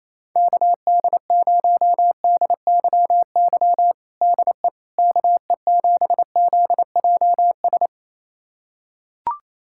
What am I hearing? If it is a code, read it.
Morse code, heard as a keyed tone, about 28 wpm, 700 Hz. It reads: KD0DYY DE KE7ZJS
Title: USA Callsign DE Callsign - 6 Characters - 28wpm